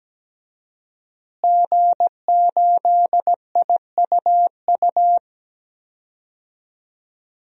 Morse code, heard as a keyed tone, about 17 wpm, 700 Hz. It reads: G8IUU
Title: DXCC-100 Call Signs - 5 Characters - Repeated 2x - 17wpm